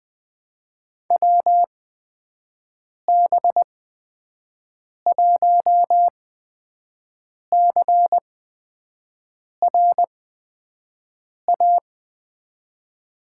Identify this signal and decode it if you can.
Morse code, heard as a keyed tone, about 20 wpm, spaced as for 5 wpm, 700 Hz. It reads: WB1CRA
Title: USA Call Signs - 6 Characters - 8x Character Spacing - 20wpm